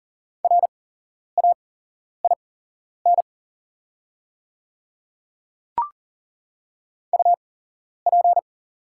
Morse code, heard as a keyed tone, about 40 wpm, spaced as for 11 wpm, 690 Hz. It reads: RAIN UP
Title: 8x Character Spacing - Top 300 Words - 40wpm